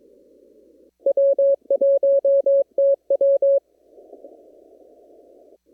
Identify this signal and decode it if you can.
Morse code, heard as a keyed tone, about 22 wpm, 550 Hz. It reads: W1TW
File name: CW / kt5j wpx cw-op-k5pi